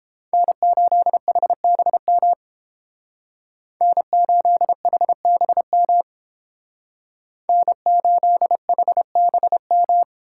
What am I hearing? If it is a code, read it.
Morse code, heard as a keyed tone, about 31 wpm, 700 Hz. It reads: N8HBM N8HBM N8HBM